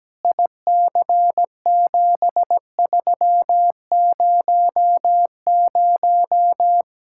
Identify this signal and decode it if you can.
Morse code, heard as a keyed tone, about 17 wpm, 700 Hz. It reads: IC7300
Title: Speed-Racing Common QSO Phrases Encore - 17wpm